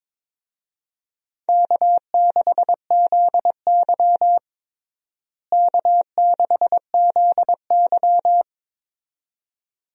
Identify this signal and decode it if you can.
Morse code, heard as a keyed tone, about 22 wpm, 700 Hz. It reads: K6ZY K6ZY
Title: USA Call Signs - 4 Characters - Repeated 2x - 22wpm